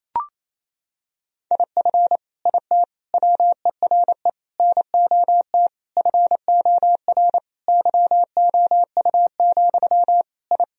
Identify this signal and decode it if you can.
Morse code, heard as a keyed tone, about 28 wpm, 700 Hz. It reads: IF IT WERE NOT FOR YOU, I
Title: Sentences from Top 1000 Words - Review Part 2 - 28wpm